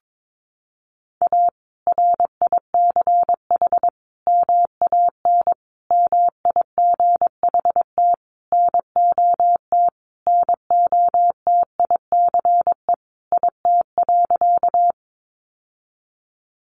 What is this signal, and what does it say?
Morse code, heard as a keyed tone, about 22 wpm, 700 Hz. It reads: A RICH MAN MIGHT NOT NOTICE IT.
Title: Sentences from Top 1100 Words 22wpm